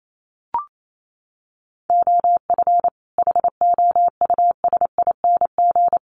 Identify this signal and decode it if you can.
Morse code, heard as a keyed tone, about 28 wpm, 700 Hz. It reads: OF HOUSING
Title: Sets of 5 Words Encore2 28wpm